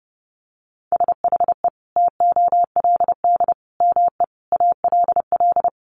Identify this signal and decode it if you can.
Morse code, heard as a keyed tone, about 30 wpm, 700 Hz. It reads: SHE TOLD ME ALL
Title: Sentences from Top 400 Words 30wpm